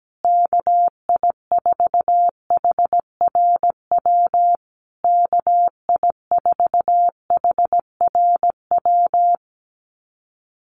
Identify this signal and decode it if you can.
Morse code, heard as a keyed tone, about 17 wpm, 700 Hz. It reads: KI4HRW KI4HRW
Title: USA Call Signs Repeated 3x 17wpm